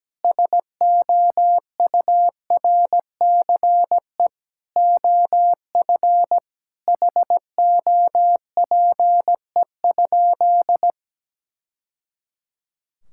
Morse code, heard as a keyed tone, about 17 wpm, 700 Hz. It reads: SOURCE OF HOPE?
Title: Sentences from Top 1000 Words - Review Part 5 - 17wpm